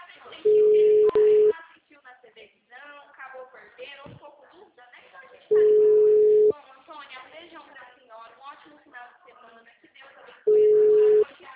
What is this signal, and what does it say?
Ringback tone (Europe, 425 Hz), with speech.